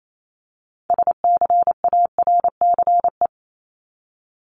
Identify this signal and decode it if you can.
Morse code, heard as a keyed tone, about 28 wpm, 700 Hz. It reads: SCARCE
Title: LY Suffix - Top 100 Words with Base Words 28wpm